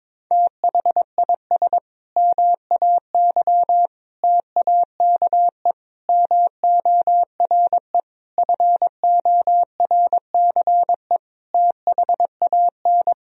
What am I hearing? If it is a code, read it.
Morse code, heard as a keyed tone, about 23 wpm, 700 Hz. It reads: THIS MAY TAKE MORE FORCE THAN
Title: Sentences from Top 1000 Words - Review Part 3 - 22wpm